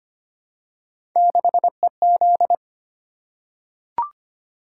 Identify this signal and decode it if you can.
Morse code, heard as a keyed tone, about 25 wpm, 700 Hz. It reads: 6EZ
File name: Three Letter-Number Combinations - 25wpm